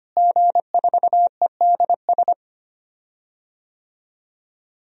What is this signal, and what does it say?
Morse code, heard as a keyed tone, about 25 wpm, 700 Hz. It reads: G4EDS
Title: DXCC-100 Call Signs - 5 Characters - Repeated 2x - 25wpm